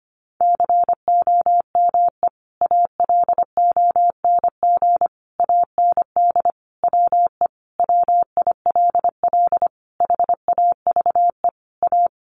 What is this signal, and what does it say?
Morse code, heard as a keyed tone, about 25 wpm, 700 Hz. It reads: COME ALONG AND WE WILL HAVE A
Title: Sentences from Top 1500 Words 25wpm